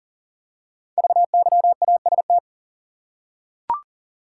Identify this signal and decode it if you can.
Morse code, heard as a keyed tone, about 42 wpm, 700 Hz. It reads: VYAST
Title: Code Groups - 5 Letters - 40wpm